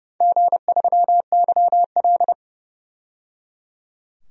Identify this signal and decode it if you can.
Morse code, heard as a keyed tone, about 31 wpm, 700 Hz. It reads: G3YL
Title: DXCC-100 Call Signs - 4 Characters - 30wpm